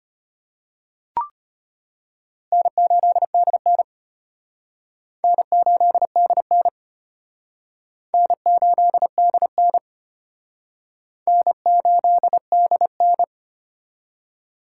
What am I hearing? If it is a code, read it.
Morse code, heard as a keyed tone, about 30 wpm, 700 Hz. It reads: N8DN N8DN N8DN N8DN
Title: Speed Racing - USA 1x2 Call Signs 25wpm